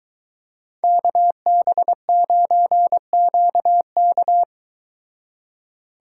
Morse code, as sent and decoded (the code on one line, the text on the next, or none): -.- -... ----. --.- -.-
KB9QK